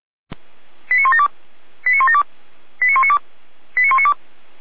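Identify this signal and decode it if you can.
Five-tone selective call (ZVEI), heard as quick runs of five tones, tones of about 70 ms. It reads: 89172 89172 89192 89192
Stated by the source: ZVEI five-tone selective call received on an SDR.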